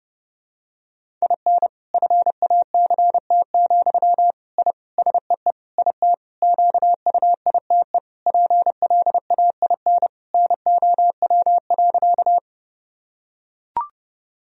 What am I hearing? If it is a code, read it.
Morse code, heard as a keyed tone, about 30 wpm, 700 Hz. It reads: IN FACT, I SEE IT QUITE PLAIN NOW.
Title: Sentences from Top 1000 Words - Review Part 3 - 30wpm